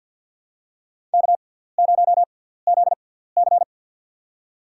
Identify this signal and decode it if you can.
Morse code, heard as a keyed tone, about 47 wpm, spaced as for 17 wpm, 700 Hz. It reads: K0ZC
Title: USA Call Signs - 6x Character Spacing - 50wpm